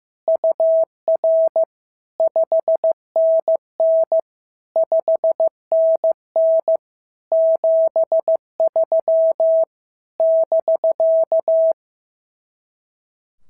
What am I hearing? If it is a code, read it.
Morse code, heard as a keyed tone, about 14 wpm, 650 Hz. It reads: UR 5NN 5NN 73 <BK>
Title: SOTA QSO Practice 15wpm